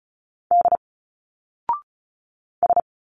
Morse code, heard as a keyed tone, about 36 wpm, 700 Hz. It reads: D S
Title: Single Letters 35wpm